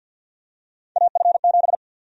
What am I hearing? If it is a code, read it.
Morse code, heard as a keyed tone, about 46 wpm, 700 Hz. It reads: AF7
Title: USA Call Sign Prefixes - Repeated 2x - 50wpm